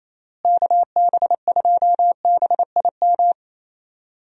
Morse code, heard as a keyed tone, about 27 wpm, 700 Hz. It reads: KB2BIM
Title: USA Call Signs - 6 Characters - Repeated 2x - 28wpm